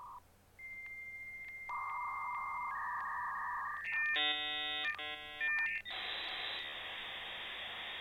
Dial-up modem handshake.